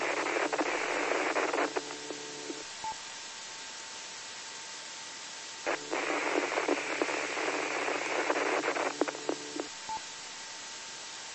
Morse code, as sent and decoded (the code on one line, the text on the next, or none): none